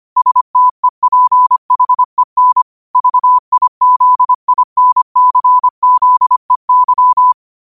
Morse code, heard as a keyed tone, about 25 wpm, 1000 Hz. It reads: STEPHEN VIZINCZEY
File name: qotdfiles / 25wpm